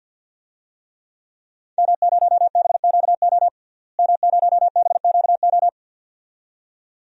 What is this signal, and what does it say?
Morse code, heard as a keyed tone, about 50 wpm, 700 Hz. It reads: M0BQO M0BQO
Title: DXCC-100 Call Signs - 5 Characters - Repeated 2x - 50wpm